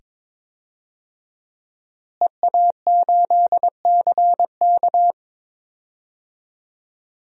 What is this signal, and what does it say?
Morse code, heard as a keyed tone, about 22 wpm, 700 Hz. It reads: EA8CK
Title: DXCC-100 Call Signs - 5 Characters - Repeated 2x - 22wpm